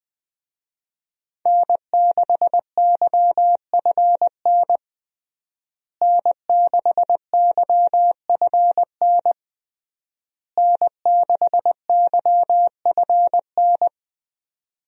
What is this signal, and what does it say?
Morse code, heard as a keyed tone, about 20 wpm, 700 Hz. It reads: N6YFN N6YFN N6YFN
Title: USA Call Signs - 5 Characters - Repeated 3x - 20wpm